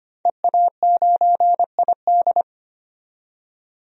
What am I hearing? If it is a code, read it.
Morse code, heard as a keyed tone, about 25 wpm, 700 Hz. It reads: EA9ID
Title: DXCC-100 Call Signs - 5 Characters - 25wpm